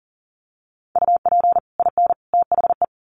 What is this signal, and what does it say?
Morse code, heard as a keyed tone, about 40 wpm, 700 Hz. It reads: UP IN THE
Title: Sets of 3 Words 40wpm